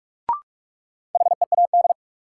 Morse code, heard as a keyed tone, about 43 wpm, 700 Hz. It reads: HEAD
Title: Sets of 3 Words - New Word Drill - 45wpm